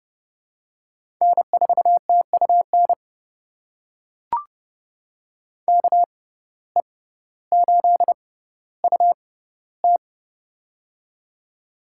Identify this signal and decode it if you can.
Morse code, heard as a keyed tone, about 30 wpm, spaced as for 15 wpm, 700 Hz. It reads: N4TUN KE8UT
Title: USA Call Signs - 5 Characters - 6x Character Spacing - 30wpm